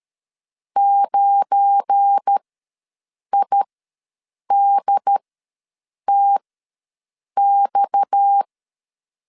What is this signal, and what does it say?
Morse code, heard as a keyed tone, about 13 wpm, spaced as for 7 wpm, 800 Hz. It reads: 9IDTX
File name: LEZIONI / 9 2 ws 12wpm 3406